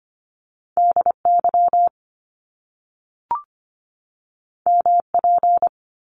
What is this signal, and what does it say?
Morse code, heard as a keyed tone, about 25 wpm, 700 Hz. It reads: D Y M P
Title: Top Two Letter Combinations in Words 25wpm